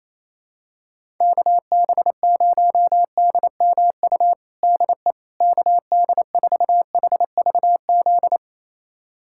Morse code, heard as a keyed tone, about 28 wpm, 700 Hz. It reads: KB0DMU DE KD4HVZ
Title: USA Callsign DE Callsign 28wpm